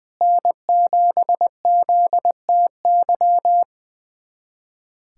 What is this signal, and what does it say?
Morse code, heard as a keyed tone, about 20 wpm, 700 Hz. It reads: N7ZTY
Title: USA Callsign DE Callsign - 5 Characters - 20wpm